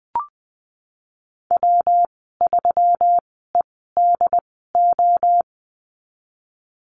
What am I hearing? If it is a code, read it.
Morse code, heard as a keyed tone, about 19 wpm, spaced as for 14 wpm, 700 Hz. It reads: W3EDO
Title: 186 Lesson 30g - Letter b - Call Signs 20wpm